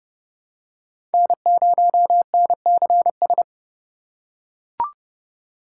Morse code, heard as a keyed tone, about 30 wpm, 700 Hz. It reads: N0NCS
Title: USA Call Signs - 5 Characters - 4x Character Spacing - 30wpm